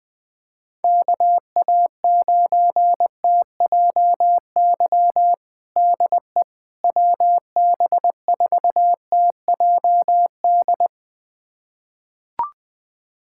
Morse code, heard as a keyed tone, about 20 wpm, 700 Hz. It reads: KA9TJY DE WB4TJD
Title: USA Callsign DE Callsign 20wpm